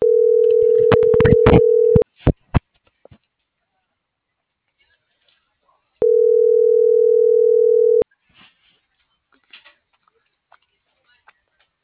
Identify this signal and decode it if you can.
Ringback tone (North America), 440 + 480 Hz.